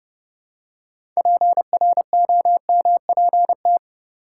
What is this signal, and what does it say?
Morse code, heard as a keyed tone, about 30 wpm, 700 Hz. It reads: PROMPT
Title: Six Letter Words - 30wpm